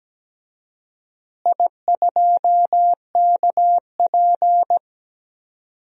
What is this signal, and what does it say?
Morse code, heard as a keyed tone, about 17 wpm, 700 Hz. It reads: I2KP
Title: DXCC-100 Call Signs - 4 Characters - Repeated 3x - 17wpm